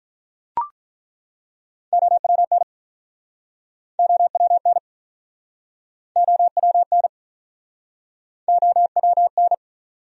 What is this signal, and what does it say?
Morse code, heard as a keyed tone, about 49 wpm, 700 Hz. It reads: OWN OWN OWN OWN